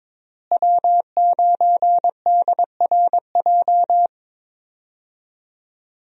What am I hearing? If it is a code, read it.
Morse code, heard as a keyed tone, about 21 wpm, 700 Hz. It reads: W9DRJ
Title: USA Call Signs - 5 Characters - Repeated 3x - 22wpm